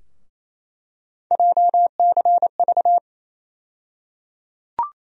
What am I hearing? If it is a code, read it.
Morse code, heard as a keyed tone, about 28 wpm, 700 Hz. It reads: JCV
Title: USA Call Sign Postfixes - 3 Letters 28wpm